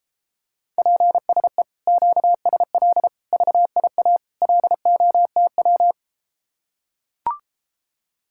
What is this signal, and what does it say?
Morse code, heard as a keyed tone, about 33 wpm, 700 Hz. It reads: PSE QSL VIA LOTW